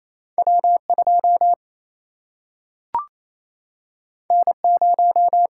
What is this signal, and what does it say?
Morse code, heard as a keyed tone, about 28 wpm, 700 Hz. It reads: W 2 N 0
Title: Single-Letter Single-Number 28wpm